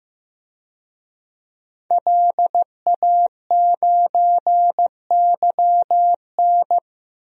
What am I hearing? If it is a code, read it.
Morse code, heard as a keyed tone, about 15 wpm, 700 Hz. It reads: LA9YN
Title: DXCC-100 Call Signs - 5 Characters - Repeated 2x - 15wpm